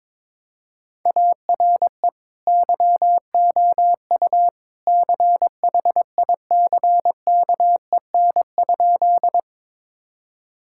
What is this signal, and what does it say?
Morse code, heard as a keyed tone, about 22 wpm, 700 Hz. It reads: ARE YOU CHICKEN?
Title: Sentences from Top 1000 Words - Review Part 5 - 22wpm